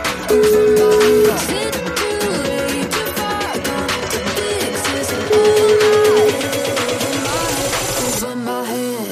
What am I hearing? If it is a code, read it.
Ringback tone (Europe, 425 Hz), with noise.